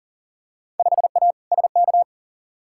Morse code, heard as a keyed tone, about 40 wpm, 700 Hz. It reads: 5A SK
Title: Field Day Exchanges Encore - 40wpm